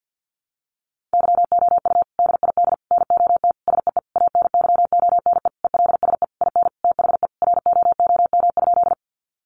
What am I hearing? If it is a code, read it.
Morse code, heard as a keyed tone, about 50 wpm, 700 Hz. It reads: YOU DID NOT SEE ANYONE ELSE IN THE ROOM?
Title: Sentences from Top 900 Words 50wpm